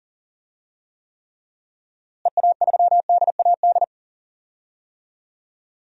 Morse code, heard as a keyed tone, about 40 wpm, 700 Hz. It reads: EA3DAD
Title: DXCC-100 Call Signs - 6 Characters - Repeated 3x - 40wpm